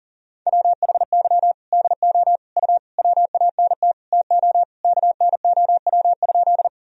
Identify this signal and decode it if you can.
Morse code, heard as a keyed tone, about 40 wpm, 700 Hz. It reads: WHY DO U WANT TO KNOW?
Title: Sentences from Top 200 Words and Abbreviations 40wpm